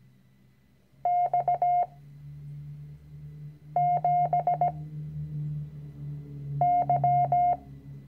Morse code, heard as a keyed tone, about 17 wpm, 710 Hz. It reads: X 7 Y
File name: CW-Online-Kurs / Anfänger / Lektion 010